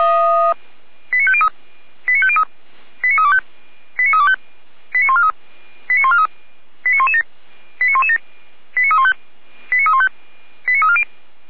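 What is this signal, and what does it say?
Five-tone selective call (ZVEI), heard as quick runs of five tones, tones of about 70 ms. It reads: 89503 89503 89325 89325 89143 89143 89117 89117 89315 89315 89340
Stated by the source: ZVEI five-tone selective call 89024 received on SDR.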